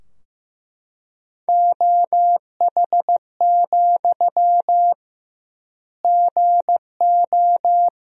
Morse code, heard as a keyed tone, about 15 wpm, 700 Hz. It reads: OH, GO
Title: 2x Word-Spacing - Sentences from Top 500 Words - 15wpm